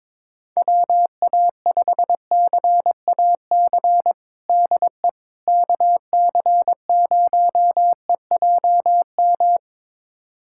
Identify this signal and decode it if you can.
Morse code, heard as a keyed tone, about 22 wpm, 700 Hz. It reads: WA5CAC DE KC0EJM
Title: USA Callsign DE Callsign 22wpm